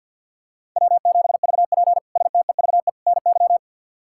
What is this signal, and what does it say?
Morse code, heard as a keyed tone, about 51 wpm, 690 Hz. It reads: W7VP STEVE NY
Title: K1USN SST QSO Practice - 50wpm